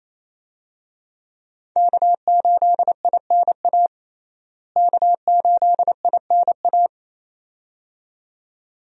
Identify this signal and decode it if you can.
Morse code, heard as a keyed tone, about 28 wpm, 700 Hz. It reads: K8INA K8INA
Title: USA Call Signs - 5 Characters - Repeated 2x - 28wpm